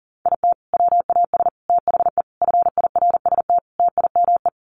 Morse code, heard as a keyed tone, about 37 wpm, 700 Hz. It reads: IT WAS THE FIRST TIME
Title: Sets of 5 Words 40wpm